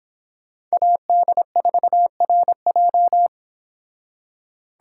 Morse code, heard as a keyed tone, about 30 wpm, 700 Hz. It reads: AD4RJ